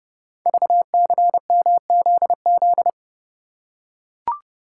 Morse code, heard as a keyed tone, about 30 wpm, 700 Hz. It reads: VCMZZ
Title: Code Groups - 5 Letters - 30wpm